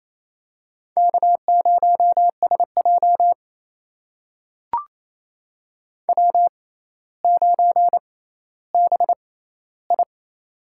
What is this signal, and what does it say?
Morse code, heard as a keyed tone, about 28 wpm, spaced as for 14 wpm, 700 Hz. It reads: K0SJ W9BI
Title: USA Call Signs - 4 Characters - 6x Character Spacing - 28wpm